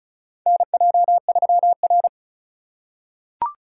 Morse code, heard as a keyed tone, about 36 wpm, 700 Hz. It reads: NJ3R
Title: USA Call Signs - 4 Characters - 4x Character Spacing - 35wpm